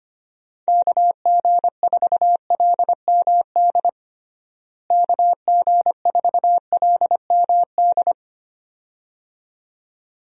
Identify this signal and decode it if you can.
Morse code, heard as a keyed tone, about 25 wpm, 700 Hz. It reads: KG4LMD KG4LMD
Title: USA Call Signs - 6 Characters - Repeated 2x - 25wpm